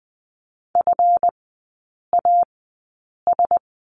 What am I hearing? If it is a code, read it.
Morse code, heard as a keyed tone, about 20 wpm, spaced as for 8 wpm, 700 Hz. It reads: FAS
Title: 138 Lesson 23e - Letter f - Three Characters 20wpm